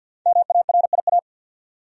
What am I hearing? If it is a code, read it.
Morse code, heard as a keyed tone, about 52 wpm, 700 Hz. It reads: MARIA
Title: USA Common Names - 50wpm